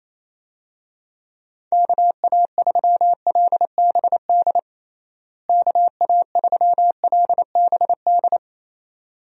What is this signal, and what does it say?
Morse code, heard as a keyed tone, about 28 wpm, 700 Hz. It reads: KA3LBD KA3LBD
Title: USA Call Signs - 6 Characters - Repeated 2x - 28wpm